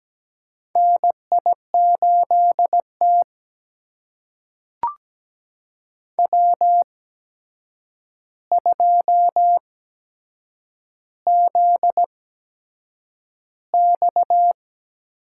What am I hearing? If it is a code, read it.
Morse code, heard as a keyed tone, about 17 wpm, spaced as for 7 wpm, 700 Hz. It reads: NI8T W2ZX
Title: USA Call Signs - 4 Characters - 8x Character Spacing - 17wpm